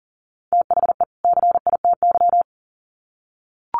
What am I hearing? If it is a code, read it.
Morse code, heard as a keyed tone, about 40 wpm, 700 Hz. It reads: THE CITY
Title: Sets of 2 Words 40wpm